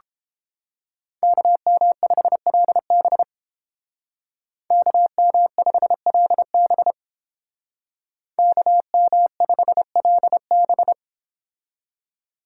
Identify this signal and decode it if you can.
Morse code, heard as a keyed tone, about 29 wpm, 700 Hz. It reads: KM5LB KM5LB KM5LB